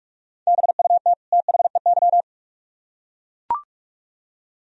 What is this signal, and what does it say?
Morse code, heard as a keyed tone, about 41 wpm, 700 Hz. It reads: BUT THEY